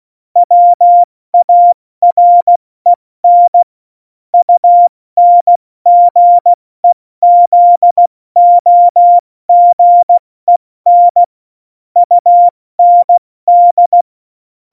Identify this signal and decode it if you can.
Morse code, heard as a keyed tone, about 16 wpm, 700 Hz. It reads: WAREN UNGEZOGEN UND
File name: Morsekurs / Witze / Witz 22. August 17, Tempo 70 F80